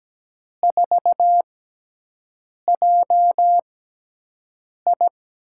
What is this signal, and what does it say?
Morse code, heard as a keyed tone, about 17 wpm, 700 Hz. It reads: 4 J I
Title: USA Call Signs - 6 Characters - 6x Character Spacing - 17wpm